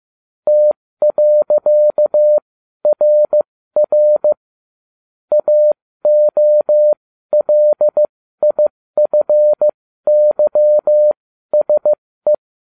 Morse code, heard as a keyed tone, about 15 wpm, spaced as for 12 wpm, 600 Hz. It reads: T.RR AOLIFYSE